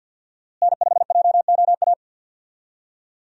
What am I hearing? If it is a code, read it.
Morse code, heard as a keyed tone, about 52 wpm, 700 Hz. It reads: N5JOA
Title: USA Call Signs - 5 Characters - Repeated 3x - 50wpm